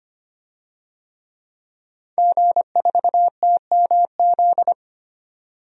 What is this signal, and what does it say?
Morse code, heard as a keyed tone, about 25 wpm, 700 Hz. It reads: G4TMZ
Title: DXCC-100 Call Signs - 5 Characters - 25wpm